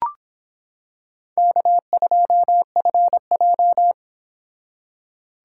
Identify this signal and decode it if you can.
Morse code, heard as a keyed tone, about 26 wpm, 700 Hz. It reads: K2FJ